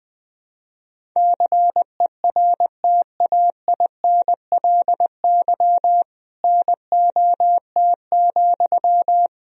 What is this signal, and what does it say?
Morse code, heard as a keyed tone, about 21 wpm, 700 Hz. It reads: CERTAINLY NOT,
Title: Sentences from Top 1000 Words - Review Part 1 - 20wpm